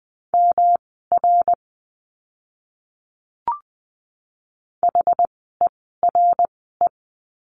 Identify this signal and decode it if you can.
Morse code, heard as a keyed tone, about 20 wpm, spaced as for 15 wpm, 700 Hz. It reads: MR HERE
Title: 115 Lesson 19f - Letter w - Words 20wpm